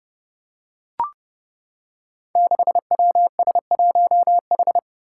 Morse code, heard as a keyed tone, about 32 wpm, 700 Hz. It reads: 6WS1H
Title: Five Challenging Letter-Number Combinations 30wpm